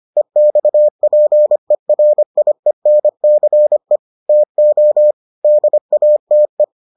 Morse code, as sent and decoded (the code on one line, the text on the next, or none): . -..- .--. .-. .. -. -.-. / - --- / -.. .- - .
EXPERIENCE TO DATE